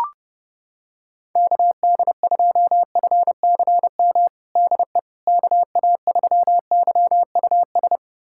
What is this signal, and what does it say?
Morse code, heard as a keyed tone, about 30 wpm, 700 Hz. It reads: KD2FCM DE KA3YUS
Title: USA Callsign DE Callsign - 6 Characters - 30wpm